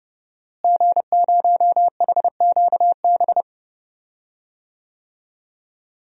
Morse code, heard as a keyed tone, about 32 wpm, 700 Hz. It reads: G0HQB
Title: DXCC-100 Call Signs - 5 Characters - Repeated 3x - 30wpm